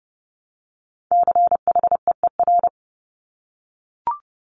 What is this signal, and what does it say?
Morse code, heard as a keyed tone, about 30 wpm, 700 Hz. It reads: CHEER
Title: Multi-letter Phonemes Repeated 3x 30wpm